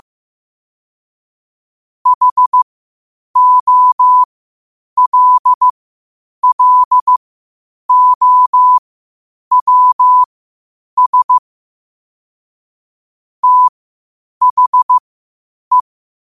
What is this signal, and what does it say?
Morse code, heard as a keyed tone, about 15 wpm, spaced as for 8 wpm, 1000 Hz. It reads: HOLLOWS THE
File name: qotdfiles / 8wpm